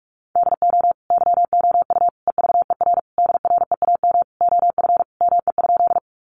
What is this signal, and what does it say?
Morse code, heard as a keyed tone, about 44 wpm, 700 Hz. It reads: DO YOU EVER DREAM OF ME?
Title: Sentences from Top 1500 Words 45wpm